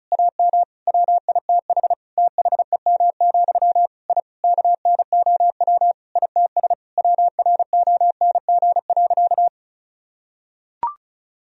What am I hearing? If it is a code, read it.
Morse code, heard as a keyed tone, about 34 wpm, 700 Hz. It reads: AM WITH THEM, I KNOW ITS WRONG.
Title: Sentences from Top 1000 Words - Review Part 1 - 35wpm